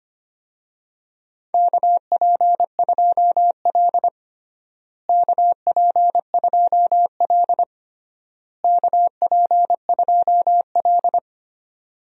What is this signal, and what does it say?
Morse code, heard as a keyed tone, about 25 wpm, 700 Hz. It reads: KP2L KP2L KP2L
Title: DXCC-100 Call Signs - 4 Characters - Repeated 3x - 25wpm